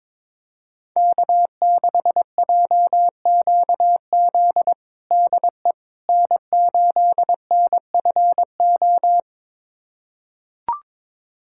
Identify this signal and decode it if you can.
Morse code, heard as a keyed tone, about 22 wpm, 700 Hz. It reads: K6JQZ DE N8NFO
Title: USA Callsign DE Callsign - 5 Characters - 22wpm